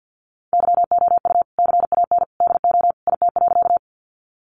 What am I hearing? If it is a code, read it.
Morse code, heard as a keyed tone, about 50 wpm, 700 Hz. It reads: YOU CAN DO IT.
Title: Sentences from Top 100 Words 50wpm